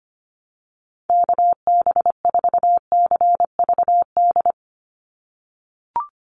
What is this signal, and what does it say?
Morse code, heard as a keyed tone, about 25 wpm, 700 Hz. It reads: KB4CVD
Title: USA Call Signs 25wpm